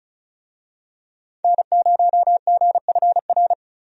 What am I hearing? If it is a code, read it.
Morse code, heard as a keyed tone, about 34 wpm, 700 Hz. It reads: N0GFR
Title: USA Call Signs - 5 Characters - 35wpm